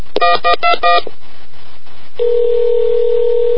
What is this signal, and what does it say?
DTMF, heard as telephone touch tones, tones of about 160 ms. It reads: 1121